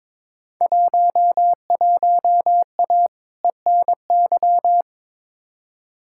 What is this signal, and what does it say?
Morse code, heard as a keyed tone, about 23 wpm, 700 Hz. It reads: 11A ENY
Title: Field Day Exchanges Encore - 22wpm